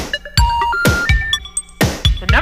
Special information tone, with noise.